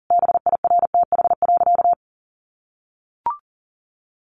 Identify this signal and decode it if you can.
Morse code, heard as a keyed tone, about 40 wpm, 700 Hz. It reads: BIRTH.
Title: Sentences from Top 900 Words 40wpm